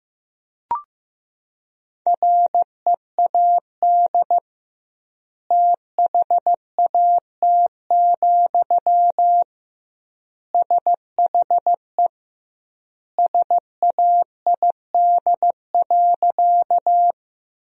Morse code, heard as a keyed tone, about 16 wpm, 700 Hz. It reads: READ THAT, SHE SAID.
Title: 2x Word-Spacing - Sentences from Top 200 Words - 15wpm